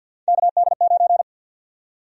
Morse code, heard as a keyed tone, about 48 wpm, 700 Hz. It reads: KD9
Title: USA Call Sign Prefixes - 50wpm